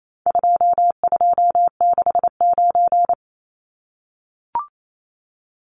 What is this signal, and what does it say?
Morse code, heard as a keyed tone, about 28 wpm, 700 Hz. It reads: 2269
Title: Numbers - 4-Digit 28wpm